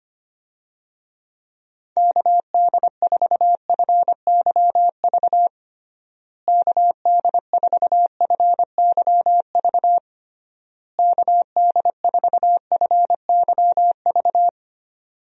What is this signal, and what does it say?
Morse code, heard as a keyed tone, about 25 wpm, 700 Hz. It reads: KD4FYV KD4FYV KD4FYV
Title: USA Call Signs - 6 Characters - Repeated 3x - 25wpm